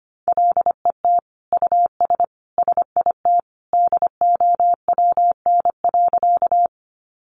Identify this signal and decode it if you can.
Morse code, heard as a keyed tone, about 25 wpm, 700 Hz. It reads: LET US SIT DOWN.
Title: Sentences from Top 500 Words 25wpm